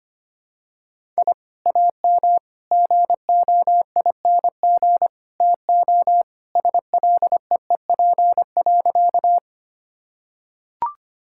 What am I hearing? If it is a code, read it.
Morse code, heard as a keyed tone, about 25 wpm, 700 Hz. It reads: I AM GOING TO SLEEP.
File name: practice / 25wpm / Sentences with 2-Word Phrases - 25wpm